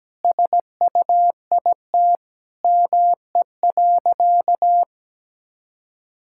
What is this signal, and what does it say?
Morse code, heard as a keyed tone, about 17 wpm, 700 Hz. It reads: SUIT ME.
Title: Sentences from Top 1600 Words 17wpm